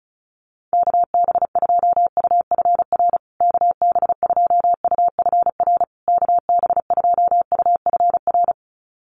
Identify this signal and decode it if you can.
Morse code, heard as a keyed tone, about 34 wpm, 700 Hz. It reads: KB2UFR KB2UFR KB2UFR
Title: USA Call Signs Repeated 3x 35wpm